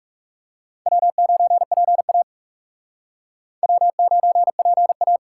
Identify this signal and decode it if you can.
Morse code, heard as a keyed tone, about 38 wpm, 700 Hz. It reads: W9PA W9PA